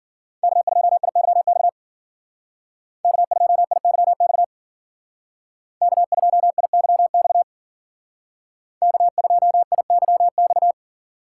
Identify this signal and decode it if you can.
Morse code, heard as a keyed tone, about 52 wpm, 700 Hz. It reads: K2IYX K2IYX K2IYX K2IYX